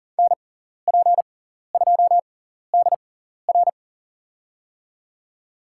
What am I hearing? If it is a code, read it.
Morse code, heard as a keyed tone, about 40 wpm, spaced as for 14 wpm, 700 Hz. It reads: NP2DR
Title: USA Call Signs - 5 Characters - 6x Character Spacing - 40wpm